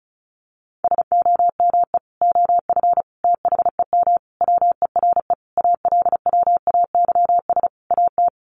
Morse code, heard as a keyed tone, about 34 wpm, 700 Hz. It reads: SOME OF THEM WERE ALWAYS AT
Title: Sentences from Top 1200 Words 35wpm